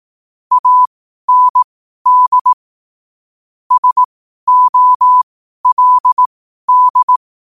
Morse code, heard as a keyed tone, about 18 wpm, spaced as for 12 wpm, 1000 Hz. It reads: AND SOLD
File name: qotdfiles / 12wpm